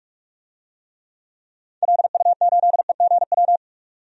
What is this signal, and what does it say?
Morse code, heard as a keyed tone, about 46 wpm, 700 Hz. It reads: LU8EGW
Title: DXCC-100 Call Signs - 6 Characters - 45wpm